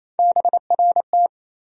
Morse code, heard as a keyed tone, about 28 wpm, 700 Hz. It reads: BRT
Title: USA Call Sign Postfixes - 3 Letters 28wpm